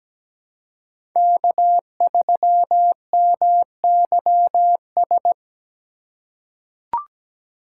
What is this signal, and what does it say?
Morse code, heard as a keyed tone, about 17 wpm, 700 Hz. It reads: K3MYS
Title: USA Call Signs - 5 Characters - 3x Character Spacing - 17wpm